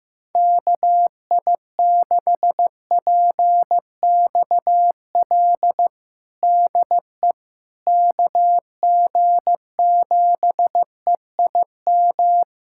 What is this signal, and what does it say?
Morse code, heard as a keyed tone, about 15 wpm, 700 Hz. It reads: KI6PXL DE KG7EIM
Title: USA Callsign DE Callsign 15wpm